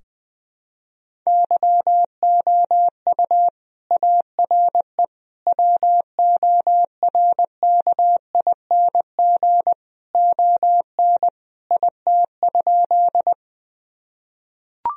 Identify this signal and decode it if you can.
Morse code, heard as a keyed tone, about 20 wpm, 700 Hz. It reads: YOU ARE WORKING ON IT?
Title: Sentences from Top 1000 Words - Review Part 4 - 20wpm